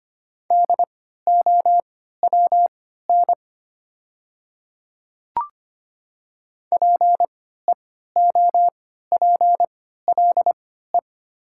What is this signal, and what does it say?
Morse code, heard as a keyed tone, about 25 wpm, spaced as for 14 wpm, 690 Hz. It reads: DOWN PEOPLE
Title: 3x Character Spacing - Top 200 Words - 25wpm